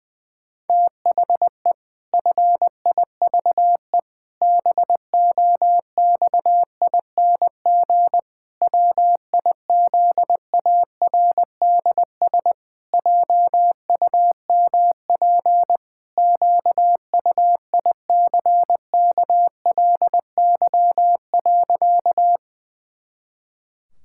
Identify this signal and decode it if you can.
Morse code, heard as a keyed tone, about 20 wpm, 700 Hz. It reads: THE FIVE BOXING WIZARDS JUMP QUICKLY.
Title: Pangram Challenge v2 - 20wpm